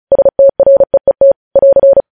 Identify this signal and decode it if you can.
Morse code, heard as a keyed tone, about 36 wpm, 550 Hz. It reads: STREET <AR>
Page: News headlines in Morse code at 35 wpm